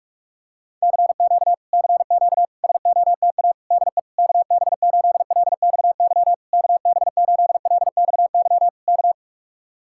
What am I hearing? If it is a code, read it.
Morse code, heard as a keyed tone, about 47 wpm, 700 Hz. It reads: CQ CQ SOTA DE KB8LXY KB8LXY K